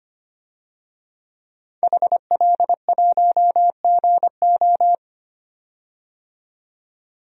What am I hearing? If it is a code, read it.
Morse code, heard as a keyed tone, about 25 wpm, 700 Hz. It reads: HL1GO
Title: DXCC-100 Call Signs - 5 Characters - 25wpm